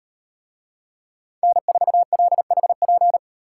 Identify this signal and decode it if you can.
Morse code, heard as a keyed tone, about 38 wpm, 700 Hz. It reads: N4LHP